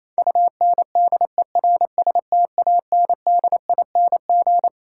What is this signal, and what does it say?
Morse code, heard as a keyed tone, about 28 wpm, 700 Hz. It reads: UNDERSTANDING
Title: Sentences from Top 1000 Words - Review Part 3 - New Word Drill - 28wpm